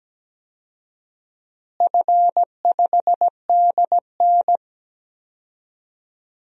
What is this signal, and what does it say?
Morse code, heard as a keyed tone, about 17 wpm, 700 Hz. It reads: F5DN
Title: DXCC-100 Call Signs - 4 Characters - Repeated 2x - 17wpm